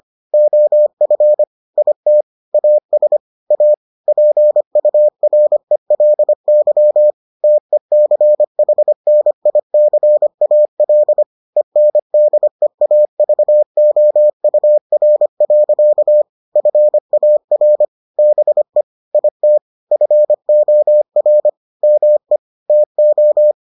Morse code, heard as a keyed tone, about 25 wpm, 600 Hz. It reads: OF IT AS A PURELY TECHNICAL ENDEAVOUR. FAR BE IT FOR ME TO